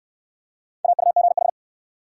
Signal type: Morse code

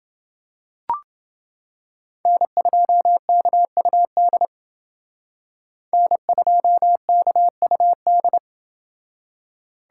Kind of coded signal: Morse code

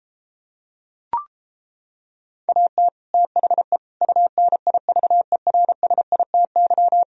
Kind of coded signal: Morse code

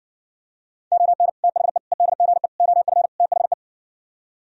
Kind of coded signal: Morse code

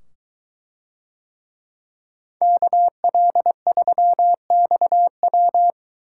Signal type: Morse code

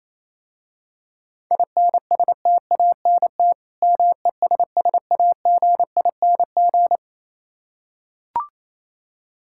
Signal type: Morse code

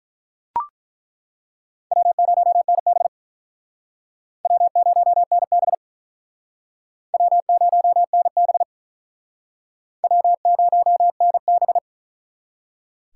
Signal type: Morse code

